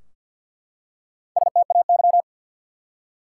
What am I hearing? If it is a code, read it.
Morse code, heard as a keyed tone, about 53 wpm, 700 Hz. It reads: STAY
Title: Four Letter Words - 50wpm